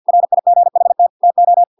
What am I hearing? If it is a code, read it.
Morse code, heard as a keyed tone, about 50 wpm, 700 Hz. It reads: RIGHT TO